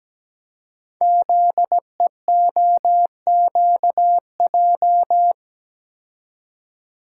Morse code, heard as a keyed tone, about 17 wpm, 700 Hz. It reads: ZEOQJ
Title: Code Groups - 5 Letters - 17wpm